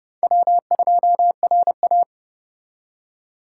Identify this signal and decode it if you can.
Morse code, heard as a keyed tone, about 36 wpm, 700 Hz. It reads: W2RA
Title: Speed Racing - USA 1x2 Call Signs 30wpm